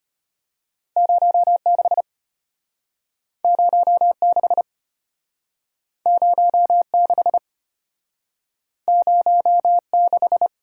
Morse code, heard as a keyed tone, about 34 wpm, 700 Hz. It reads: 06 06 06 06